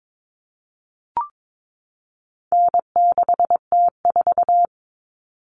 Morse code, heard as a keyed tone, about 22 wpm, 700 Hz. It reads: N6T4
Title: All Cut Numbers - 4-Digit 22wpm